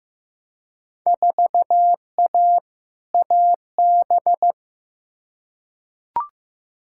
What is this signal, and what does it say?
Morse code, heard as a keyed tone, about 15 wpm, 700 Hz. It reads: 4A AB
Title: Field Day Exchanges Encore - 15wpm